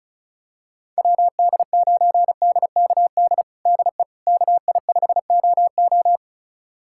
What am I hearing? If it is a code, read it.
Morse code, heard as a keyed tone, about 36 wpm, 700 Hz. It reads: WD9DKD DE KI5OO
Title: USA Callsign DE Callsign 35wpm